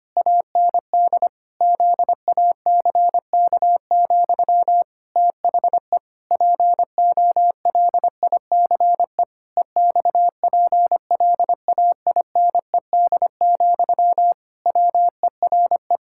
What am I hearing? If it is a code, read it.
Morse code, heard as a keyed tone, about 25 wpm, 700 Hz. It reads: AND ZACK, THE POLICE EXPLAINED, WERE
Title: Pangram Challenge v2 - 25wpm